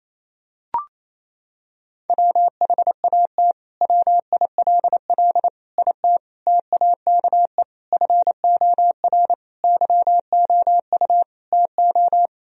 Morse code, heard as a keyed tone, about 28 wpm, 700 Hz. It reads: WHAT WILL IT TAKE FOR YOU TO
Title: Sentences from Top 1000 Words - Review Part 3 - 28wpm